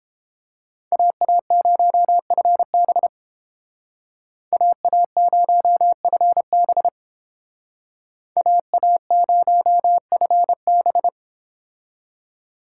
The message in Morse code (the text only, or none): AA0FB AA0FB AA0FB